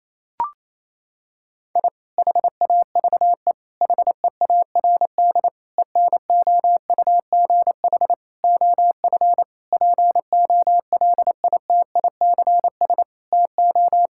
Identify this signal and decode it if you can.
Morse code, heard as a keyed tone, about 28 wpm, 700 Hz. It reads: I HAVE HEARD ENOUGH OF POLITICS TO
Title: Sentences from Top 1000 Words - Review Part 6 - 28wpm